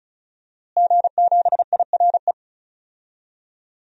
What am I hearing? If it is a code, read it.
Morse code, heard as a keyed tone, about 36 wpm, 700 Hz. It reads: G7IRE
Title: DXCC-100 Call Signs - 5 Characters - Repeated 2x - 35wpm